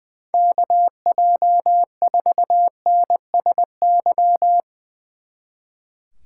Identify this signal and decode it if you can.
Morse code, heard as a keyed tone, about 22 wpm, 700 Hz. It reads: KJ4NSY
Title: Speed Racing - USA Call Signs 20wpm